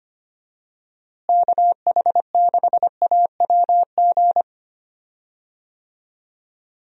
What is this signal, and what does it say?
Morse code, heard as a keyed tone, about 25 wpm, 700 Hz. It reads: KH6AWG
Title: DXCC-100 Call Signs - 6 Characters - Repeated 2x - 25wpm